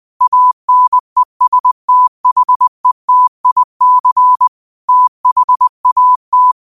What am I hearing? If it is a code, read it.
Morse code, heard as a keyed tone, about 20 wpm, 1000 Hz. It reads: ANESTHETIC THAT
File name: qotdfiles / 20wpm